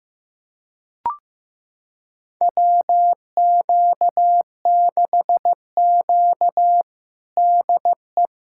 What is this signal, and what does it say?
Morse code, heard as a keyed tone, about 15 wpm, 700 Hz. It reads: WQ6Q DE
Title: USA Callsign DE Callsign - 4 Characters - 15wpm